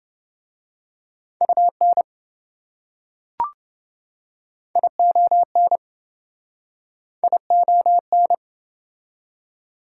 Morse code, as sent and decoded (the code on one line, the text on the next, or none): ..- -. / .. --- -. / .. --- -.
UN ION ION